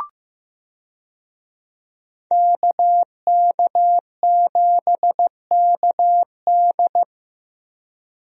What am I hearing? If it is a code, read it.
Morse code, heard as a keyed tone, about 15 wpm, 700 Hz. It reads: KK7KD
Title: USA Call Signs - 5 Characters - Repeated 3x - 15wpm